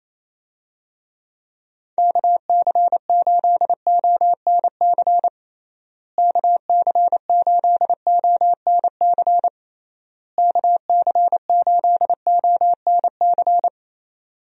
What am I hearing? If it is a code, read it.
Morse code, heard as a keyed tone, about 28 wpm, 700 Hz. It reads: KC8ONC KC8ONC KC8ONC
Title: USA Call Signs - 6 Characters - Repeated 3x - 28wpm